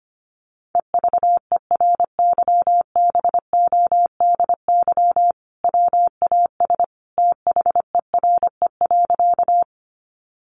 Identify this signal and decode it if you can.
Morse code, heard as a keyed tone, about 25 wpm, 700 Hz. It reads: EVERYBODY WAS THERE.
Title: Sentences from Top 1400 Words 25wpm